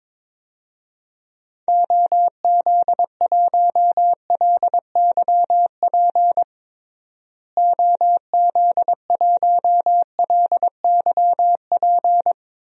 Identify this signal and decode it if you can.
Morse code, heard as a keyed tone, about 21 wpm, 700 Hz. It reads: OZ1LYP OZ1LYP
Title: DXCC-100 Call Signs - 6 Characters - Repeated 3x - 22wpm